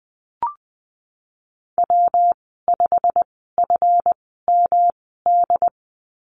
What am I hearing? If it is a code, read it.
Morse code, heard as a keyed tone, about 19 wpm, spaced as for 14 wpm, 700 Hz. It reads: W5FMD
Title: 167 Lesson 27g - Number 7 - Call Signs 20wpm